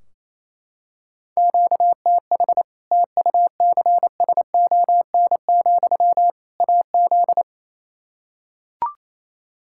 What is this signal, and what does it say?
Morse code, heard as a keyed tone, about 28 wpm, 700 Hz. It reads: QTH TUCSON, AZ
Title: Practice QSO - Exchange 1 - Beginning - 28wpm